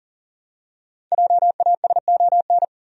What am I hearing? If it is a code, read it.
Morse code, heard as a keyed tone, about 40 wpm, 700 Hz. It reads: JASON
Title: USA Common Names - 40wpm